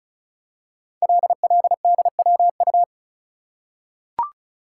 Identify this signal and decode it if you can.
Morse code, heard as a keyed tone, about 36 wpm, 700 Hz. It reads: LLDWU
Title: Code Groups - 5 Letter:Numbers - 35wpm